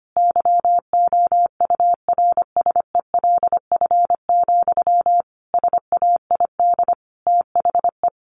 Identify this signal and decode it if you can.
Morse code, heard as a keyed tone, about 25 wpm, 700 Hz. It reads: YOURSELF, SAID THE
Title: Sentences from Top 1100 Words 25wpm